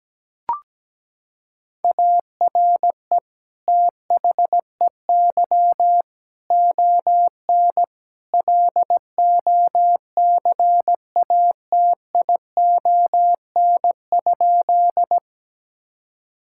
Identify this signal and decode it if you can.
Morse code, heard as a keyed tone, about 17 wpm, 700 Hz. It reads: ARE THEY ON LOCATION?
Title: Sentences from Top 1000 Words - Review Part 6 - 17wpm